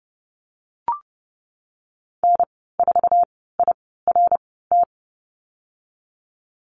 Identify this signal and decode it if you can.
Morse code, heard as a keyed tone, about 30 wpm, spaced as for 17 wpm, 700 Hz. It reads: N4IRT
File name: Lesson 14g - Number 2 - Call Signs 30wpm